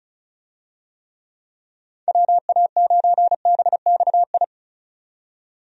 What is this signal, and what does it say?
Morse code, heard as a keyed tone, about 36 wpm, 700 Hz. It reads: WA9BXI
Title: USA Call Signs - 6 Characters - 35wpm